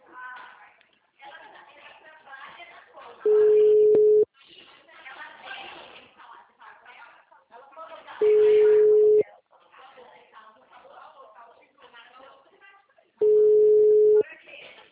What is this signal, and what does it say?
Ringback tone (Europe, 425 Hz), with speech.
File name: out- -2016- - - .27